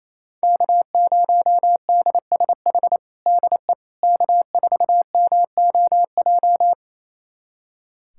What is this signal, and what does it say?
Morse code, heard as a keyed tone, about 27 wpm, 700 Hz. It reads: K0DSH DE K4MOJ
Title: USA Callsign DE Callsign - 5 Characters - 28wpm